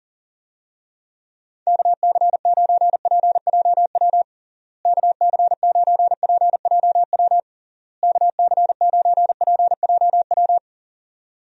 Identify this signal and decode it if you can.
Morse code, heard as a keyed tone, about 40 wpm, 700 Hz. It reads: KC9PJW KC9PJW KC9PJW
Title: USA Call Signs - 6 Characters - Repeated 3x - 40wpm